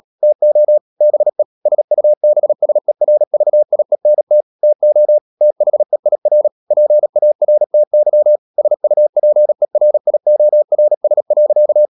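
Morse code, heard as a keyed tone, about 37 wpm, 600 Hz. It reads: TO BE SUBSERVIENT TO THEIR PARTY SUPERIORS.
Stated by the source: Morse code practice at 35–39 wpm sending English text of Ken Follett's novel Never, from CW med Gnister.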